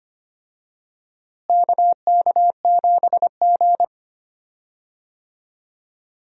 Morse code, heard as a keyed tone, about 25 wpm, 700 Hz. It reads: KK7G
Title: USA Call Signs - 4 Characters - Repeated 3x - 25wpm